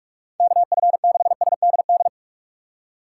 Morse code, heard as a keyed tone, about 44 wpm, 700 Hz. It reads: KF6SDD
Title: USA Call Signs - 6 Characters - 4x Character Spacing - 45wpm